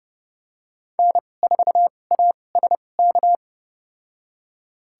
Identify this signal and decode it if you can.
Morse code, heard as a keyed tone, about 30 wpm, spaced as for 22 wpm, 700 Hz. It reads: N4ASK
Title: USA Call Signs - 2x Character Spacing - 30wpm